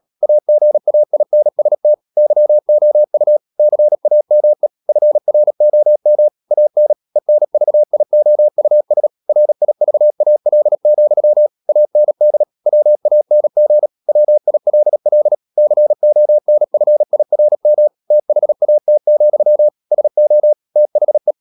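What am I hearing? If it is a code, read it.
Morse code, heard as a keyed tone, about 37 wpm, 600 Hz. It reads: AGAINST YOU CAME FROM AN ENVIOUS RIVAL, AND WANG WILL CONFIRM THAT, SO THE
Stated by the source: Morse code practice at 35–39 wpm sending English text of Ken Follett's novel Never, from CW med Gnister.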